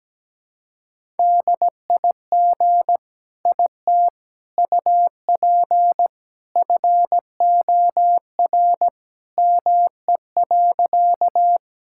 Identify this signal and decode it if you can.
Morse code, heard as a keyed tone, about 17 wpm, 700 Hz. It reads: DIG IT UP FOR ME.
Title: Sentences from Top 1600 Words 17wpm